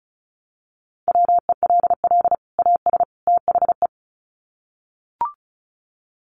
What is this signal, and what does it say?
Morse code, heard as a keyed tone, about 34 wpm, 700 Hz. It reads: WELL AS THE
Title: Sets of 3 Words 35wpm